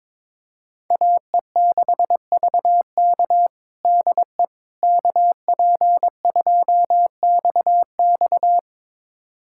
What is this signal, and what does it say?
Morse code, heard as a keyed tone, about 22 wpm, 700 Hz. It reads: AE6VK DE KP2XX
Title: USA Callsign DE Callsign - 5 Characters - 22wpm